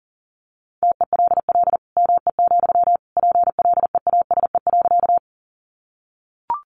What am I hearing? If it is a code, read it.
Morse code, heard as a keyed tone, about 37 wpm, 700 Hz. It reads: TELL ME, PLEASE.
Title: Sentences from Top 1500 Words 40wpm